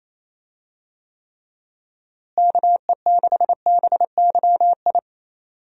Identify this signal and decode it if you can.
Morse code, heard as a keyed tone, about 28 wpm, 700 Hz. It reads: KE6BYI
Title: USA Call Signs - 6 Characters - Repeated 3x - 28wpm